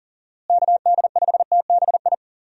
Morse code, heard as a keyed tone, about 40 wpm, 700 Hz. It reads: KD5TBI
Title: USA Call Signs - 6 Characters - Repeated 2x - 40wpm